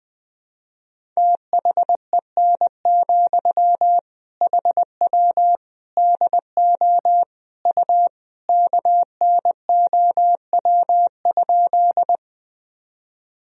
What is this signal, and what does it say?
Morse code, heard as a keyed tone, about 20 wpm, 700 Hz. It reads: THEN, HW DO U KNOW?
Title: Sentences from Top 100 Words and Abbreviations 20wpm